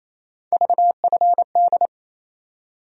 Morse code, heard as a keyed tone, about 28 wpm, 700 Hz. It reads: VFD
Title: USA Call Sign Postfixes - 3 Letters 28wpm